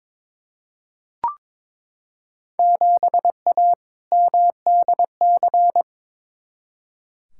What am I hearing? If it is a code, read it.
Morse code, heard as a keyed tone, about 22 wpm, 700 Hz. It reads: 7A MDC
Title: Field Day Exchanges Encore - 22wpm